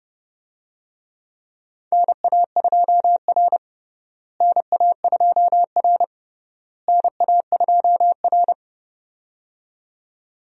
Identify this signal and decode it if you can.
Morse code, heard as a keyed tone, about 30 wpm, 700 Hz. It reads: NA2R NA2R NA2R
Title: USA Call Signs - 4 Characters - Repeated 3x - 30wpm